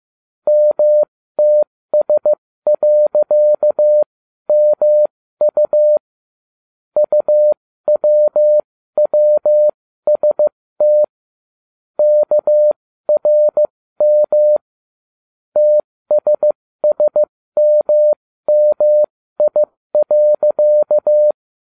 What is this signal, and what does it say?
Morse code, heard as a keyed tone, about 15 wpm, spaced as for 13 wpm, 600 Hz. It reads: MTS.MU UWWST KRM TSSMMI.